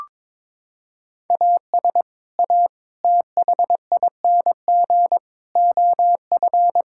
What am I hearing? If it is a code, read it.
Morse code, heard as a keyed tone, about 22 wpm, 700 Hz. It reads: AS A THING OF
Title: Sentences from Top 1000 Words - Review Part 6 - 22wpm